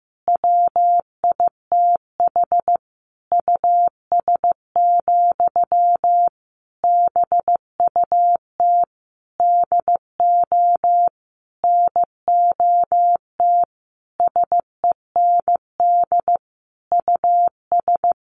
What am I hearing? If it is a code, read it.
Morse code, heard as a keyed tone, about 15 wpm, 700 Hz. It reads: WITH US, BUT DO NOT SEND US
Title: Sentences from Top 1500 Words 15wpm